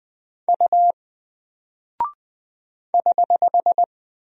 Morse code, heard as a keyed tone, about 21 wpm, 700 Hz. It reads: U <HH>
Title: Single Character (Letter:Number:Punctuation:Prosign) Alt 20wpm